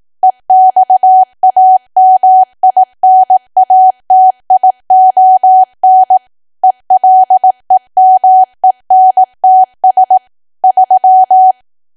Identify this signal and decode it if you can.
Morse code, heard as a keyed tone, about 18 wpm, 750 Hz. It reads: EXAMINATION ELEMENTS 3
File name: cwa / W1AW-Level2 / 18WPM